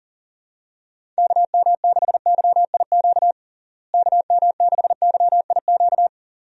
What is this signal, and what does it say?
Morse code, heard as a keyed tone, about 40 wpm, 700 Hz. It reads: KM6YIQ KM6YIQ
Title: USA Call Signs - 6 Characters - Repeated 2x - 40wpm